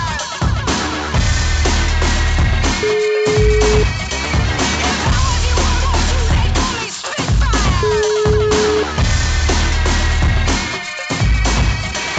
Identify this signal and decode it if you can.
Ringback tone (Europe, 425 Hz), with noise.